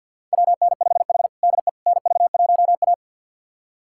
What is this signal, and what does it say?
Morse code, heard as a keyed tone, about 51 wpm, 700 Hz. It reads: WN5H DE NV1A
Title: USA Callsign DE Callsign - 4 Characters - 50wpm